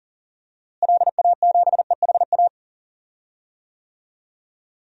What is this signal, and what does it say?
Morse code, heard as a keyed tone, about 40 wpm, 700 Hz. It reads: LA7EHA
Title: DXCC-100 Call Signs - 6 Characters - Repeated 3x - 40wpm